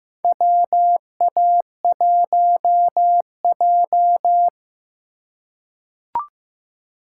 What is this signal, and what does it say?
Morse code, heard as a keyed tone, about 15 wpm, 700 Hz. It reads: WA1J
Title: USA Call Signs - 4 Characters - 4x Character Spacing - 15wpm